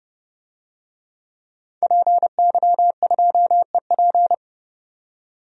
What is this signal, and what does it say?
Morse code, heard as a keyed tone, about 30 wpm, 700 Hz. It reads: PY2EP
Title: DXCC-100 Call Signs - 5 Characters - 30wpm